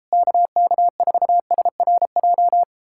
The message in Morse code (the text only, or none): KK4SRJ